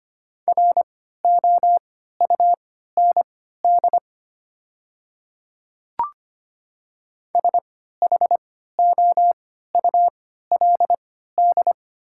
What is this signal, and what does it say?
Morse code, heard as a keyed tone, about 25 wpm, spaced as for 14 wpm, 700 Hz. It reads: ROUND SHOULD
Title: 3x Character Spacing - Top 300 Words - 25wpm